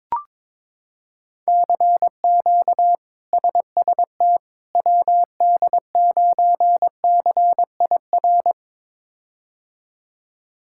Morse code, heard as a keyed tone, about 22 wpm, 700 Hz. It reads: CQ SST WD9CIR
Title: K1USN SST QSO Practice - 22wpm